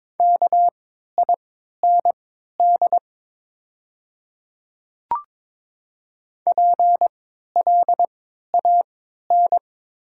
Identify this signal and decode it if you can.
Morse code, heard as a keyed tone, about 22 wpm, spaced as for 12 wpm, 690 Hz. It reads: KIND PLAN
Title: 3x Character Spacing - Top 500 Words - 22wpm